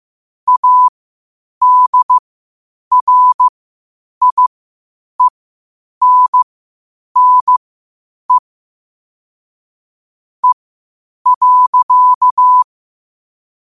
Morse code, heard as a keyed tone, about 15 wpm, spaced as for 8 wpm, 1000 Hz. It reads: ADRIENNE E.
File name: qotdfiles / 8wpm